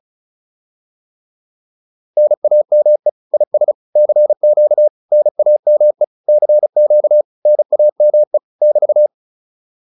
Morse code, heard as a keyed tone, about 35 wpm, 600 Hz. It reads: NAME IS CQ NAME CQ NAME <BT>
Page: advanced-practice-files